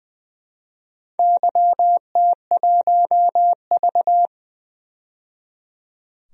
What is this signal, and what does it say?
Morse code, heard as a keyed tone, about 20 wpm, 700 Hz. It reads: YT1V
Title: DXCC-100 Call Signs - 4 Characters - 20wpm